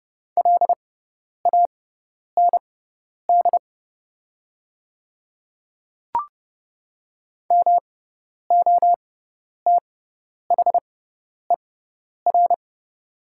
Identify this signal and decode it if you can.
Morse code, heard as a keyed tone, about 30 wpm, spaced as for 10 wpm, 700 Hz. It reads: LAND MOTHER
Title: 6x Character Spacing - Top 200 Words - 30wpm